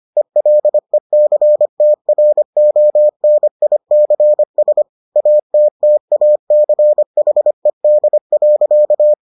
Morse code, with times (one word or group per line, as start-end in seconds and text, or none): 0.2-4.8 ELECTRONICS
5.2-9.1 ATTACHED.